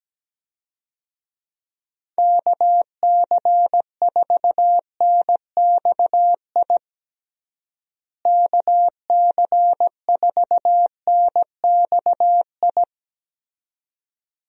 Morse code, heard as a keyed tone, about 17 wpm, 700 Hz. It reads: KC4NXI KC4NXI
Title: USA Call Signs - 6 Characters - Repeated 2x - 17wpm